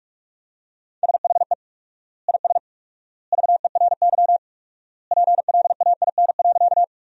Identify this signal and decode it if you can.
Morse code, heard as a keyed tone, about 43 wpm, 700 Hz. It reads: SHE IS VERY PLAIN.
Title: 4x Word-Spacing - Sentences from Top 400 Words - 45wpm